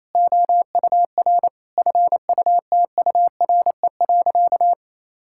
Morse code, heard as a keyed tone, about 28 wpm, 700 Hz. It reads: OUR FUTURE.
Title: Sentences from Top 1000 Words - Review Part 5 - 28wpm